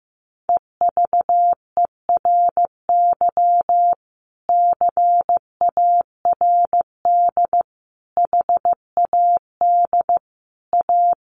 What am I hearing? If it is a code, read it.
Morse code, heard as a keyed tone, about 15 wpm, 700 Hz. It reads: EVERY CARD HAD A
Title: Sentences from Top 1400 Words 15wpm